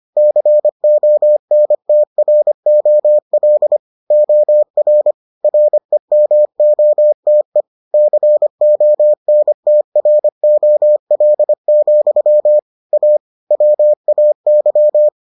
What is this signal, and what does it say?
Morse code, heard as a keyed tone, about 25 wpm, 600 Hz. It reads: CONTROL OR REMOTE CONTROL, A WAY